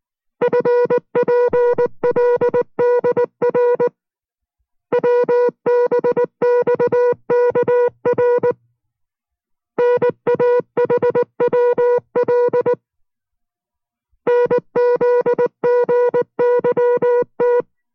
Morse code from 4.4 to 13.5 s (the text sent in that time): WBXKR NAHWL